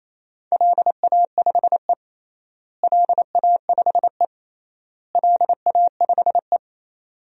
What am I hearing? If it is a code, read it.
Morse code, heard as a keyed tone, about 28 wpm, 700 Hz. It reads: LA5E LA5E LA5E
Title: DXCC-100 Call Signs - 4 Characters - Repeated 3x - 28wpm